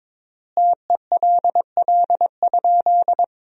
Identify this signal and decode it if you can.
Morse code, heard as a keyed tone, about 22 wpm, 700 Hz. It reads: TELL?
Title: 3x Word-Spacing - Sentences from Top 200 Words - 22wpm